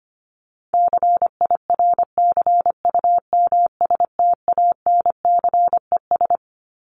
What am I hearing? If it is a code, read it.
Morse code, heard as a keyed tone, about 25 wpm, 700 Hz. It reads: CIRCUMSTANCES
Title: ES:S Suffix - Top 100 Words with Base Words 25wpm